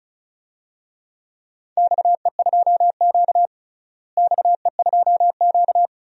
Morse code, heard as a keyed tone, about 34 wpm, 700 Hz. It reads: XE2Q XE2Q
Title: DXCC-100 Call Signs - 4 Characters - Repeated 3x - 35wpm